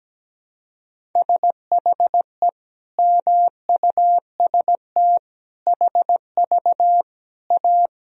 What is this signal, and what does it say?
Morse code, heard as a keyed tone, about 17 wpm, 700 Hz. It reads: SHE MUST HV A
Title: Sentences from Top 300 Words and Abbreviations 17wpm